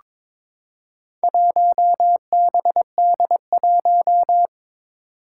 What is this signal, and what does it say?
Morse code, heard as a keyed tone, about 22 wpm, 700 Hz. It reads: 1BD1
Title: Four Challenging Letter-Number Combinations 22wpm